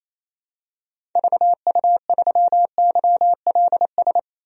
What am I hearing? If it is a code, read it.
Morse code, heard as a keyed tone, about 28 wpm, 700 Hz. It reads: VU3YLS
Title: DXCC-100 Call Signs - 6 Characters - 28wpm